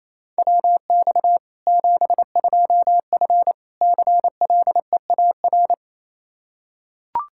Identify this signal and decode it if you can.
Morse code, heard as a keyed tone, about 28 wpm, 700 Hz. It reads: WX 72F CLEAR
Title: Practice QSO - Exchange 2 - Info - 28wpm